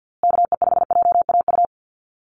Morse code, heard as a keyed tone, about 50 wpm, 700 Hz. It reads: KE5JAU
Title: USA Call Signs Repeated 2x 50wpm